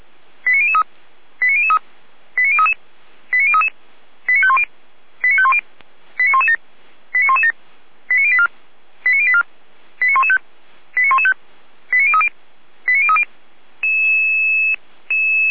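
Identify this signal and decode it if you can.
Five-tone selective call (ZVEI), heard as quick runs of five tones, tones of about 80 ms. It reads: 89003 89003 89033 89033 89510 89510 89117 89117 89084 89084 89115 89115 89030 89030
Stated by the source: Five-tone ZVEI selective call received on SDR.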